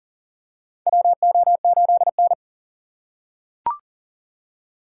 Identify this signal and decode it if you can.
Morse code, heard as a keyed tone, about 39 wpm, 700 Hz. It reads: WO8N
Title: USA Call Signs - 6x Character Spacing - 40wpm